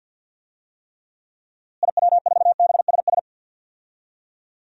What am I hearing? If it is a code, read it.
Morse code, heard as a keyed tone, about 47 wpm, 700 Hz. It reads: IW4BSS
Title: DXCC-100 Call Signs - 6 Characters - Repeated 3x - 50wpm